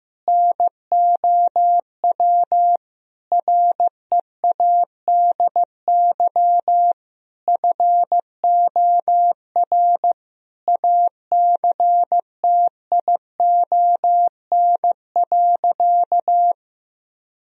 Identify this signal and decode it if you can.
Morse code, heard as a keyed tone, about 16 wpm, 700 Hz. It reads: NOW READY FOR ACTION.
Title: Sentences from Top 1000 Words - Review Part 4 - 15wpm